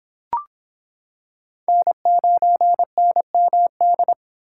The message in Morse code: N9NMD

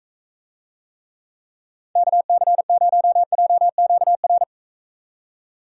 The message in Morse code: KC0JQR

